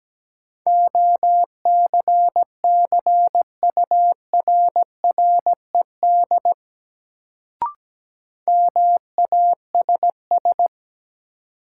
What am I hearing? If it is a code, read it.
Morse code, heard as a keyed tone, about 17 wpm, 700 Hz. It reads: OCCURRED MASS
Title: Sets of 5 Words Encore - New Word Drill - 17wpm